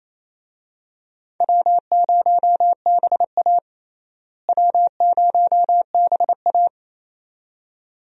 Morse code, heard as a keyed tone, about 27 wpm, 700 Hz. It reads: W0BA W0BA
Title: USA Call Signs - 4 Characters - Repeated 2x - 28wpm